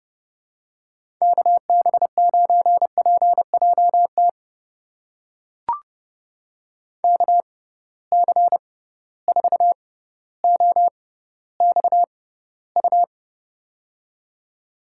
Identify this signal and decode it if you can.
Morse code, heard as a keyed tone, about 30 wpm, spaced as for 15 wpm, 700 Hz. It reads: KB9PJT KC4OXU
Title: USA Call Signs - 6 Characters - 6x Character Spacing - 30wpm